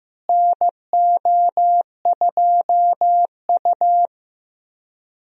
Morse code, heard as a keyed tone, about 15 wpm, 700 Hz. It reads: NO2U
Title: USA Call Signs - 4 Characters - Repeated 3x - 15wpm